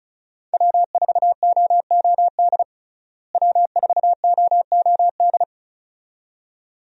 Morse code, heard as a keyed tone, about 34 wpm, 700 Hz. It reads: W4OOD W4OOD
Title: USA Call Signs - 5 Characters - Repeated 2x - 35wpm